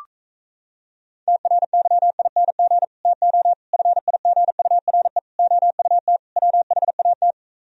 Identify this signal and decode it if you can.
Morse code, heard as a keyed tone, about 37 wpm, 700 Hz. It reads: TRYING TO FIGURE OUT WHAT